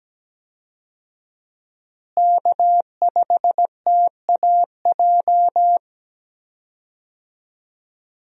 Morse code, heard as a keyed tone, about 17 wpm, 700 Hz. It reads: K5TAJ
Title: USA Call Signs - 5 Characters - Repeated 3x - 17wpm